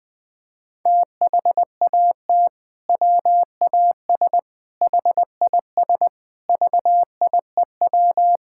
Morse code, heard as a keyed tone, about 19 wpm, 700 Hz. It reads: THAT WAS HIS VIEW
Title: Sentences from Top 1000 Words - Review Part 6 - 20wpm